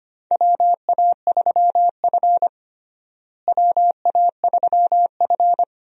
Morse code, heard as a keyed tone, about 25 wpm, 700 Hz. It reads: WA3F WA3F
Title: USA Call Signs - 4 Characters - Repeated 3x - 25wpm